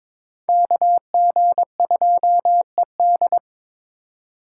Morse code, heard as a keyed tone, about 26 wpm, 700 Hz. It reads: KG2ED
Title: Speed Racing - USA 2x2 Call Signs 22wpm